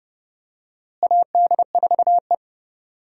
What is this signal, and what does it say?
Morse code, heard as a keyed tone, about 30 wpm, 700 Hz. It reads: AD4E
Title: USA Call Signs - 4 Characters - Repeated 2x - 30wpm